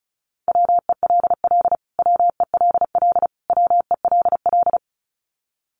Morse code, heard as a keyed tone, about 34 wpm, 700 Hz. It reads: WELL WELL WELL
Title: Top 500 Words Repeated 35wpm